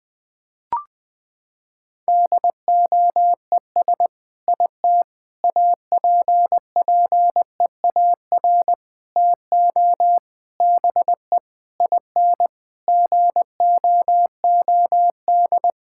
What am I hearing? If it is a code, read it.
Morse code, heard as a keyed tone, about 21 wpm, 700 Hz. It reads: DOES IT APPEAR TO BE IN GOOD
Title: Sentences from Top 1000 Words - Review Part 6 - 20wpm